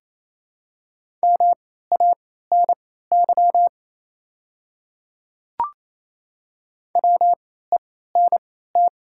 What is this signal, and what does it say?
Morse code, heard as a keyed tone, about 28 wpm, spaced as for 16 wpm, 700 Hz. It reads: MANY WENT
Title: 3x Character Spacing - Top 200 Words - 28wpm